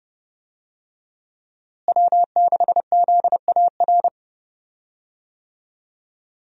Morse code, heard as a keyed tone, about 30 wpm, 700 Hz. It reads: W6ZAR
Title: USA Call Signs - 5 Characters - Repeated 2x - 30wpm